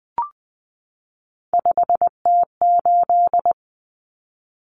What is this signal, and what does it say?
Morse code, heard as a keyed tone, about 20 wpm, 700 Hz. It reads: 5T8
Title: Common Cut Numbers - 3-Digit 20wpm